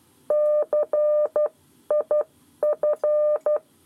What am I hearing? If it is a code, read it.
Morse code, heard as a keyed tone, about 12 wpm, spaced as for 10 wpm, 600 Hz. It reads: CIF
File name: 7 Lezione cw 1